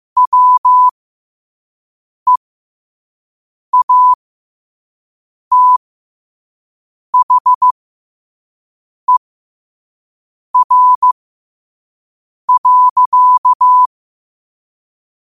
Morse code, heard as a keyed tone, about 15 wpm, spaced as for 5 wpm, 1000 Hz. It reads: WEATHER.
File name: qotdfiles / 5wpm